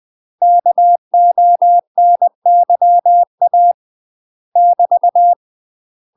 Morse code, heard as a keyed tone, about 20 wpm, 700 Hz. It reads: KONYA <BT>
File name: UK MASTER QSOs TRAINING / 20WPM / UK MASTER QSOs 0025